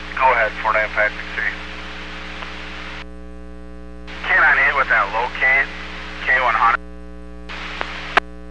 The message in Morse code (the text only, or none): none